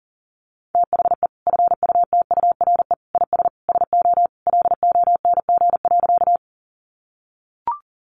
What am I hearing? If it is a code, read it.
Morse code, heard as a keyed tone, about 40 wpm, 700 Hz. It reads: THE FUTURE IS SO LONG.
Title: Sentences from Top 700 Words 40wpm